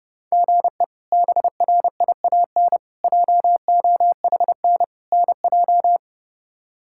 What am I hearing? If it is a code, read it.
Morse code, heard as a keyed tone, about 30 wpm, 700 Hz. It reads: GE BRIAN JOHN NJ
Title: K1USN SST QSO Practice - 30wpm